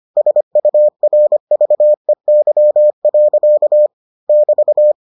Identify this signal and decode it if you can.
Morse code, heard as a keyed tone, about 25 wpm, 600 Hz. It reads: SURVEY. <BT>